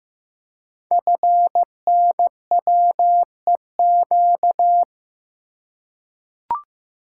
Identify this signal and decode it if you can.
Morse code, heard as a keyed tone, about 15 wpm, 700 Hz. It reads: FNWEQ
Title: Code Groups - 5 Letters - 15wpm